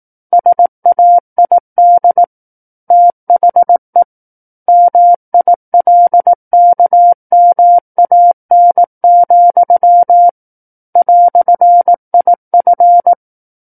Morse code, heard as a keyed tone, about 18 wpm, 700 Hz. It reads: SAID THE MILKMAN, "IF
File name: 18 WPM